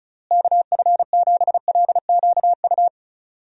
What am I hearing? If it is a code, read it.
Morse code, heard as a keyed tone, about 36 wpm, 700 Hz. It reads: KF7LQU
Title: USA Call Signs - 6 Characters - Repeated 2x - 35wpm